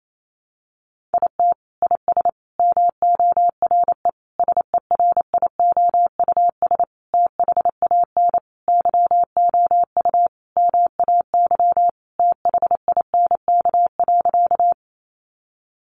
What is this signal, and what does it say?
Morse code, heard as a keyed tone, about 28 wpm, 700 Hz. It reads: IT IS MORE SERIOUS THAN YOU MAY THINK.
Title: Sentences from Top 1100 Words 28wpm